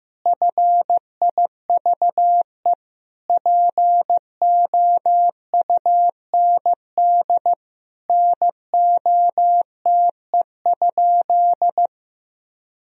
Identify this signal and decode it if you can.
Morse code, heard as a keyed tone, about 15 wpm, 700 Hz. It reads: FIVE POUND NOTE?
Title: Sentences from Top 1000 Words - Review Part 2 - 15wpm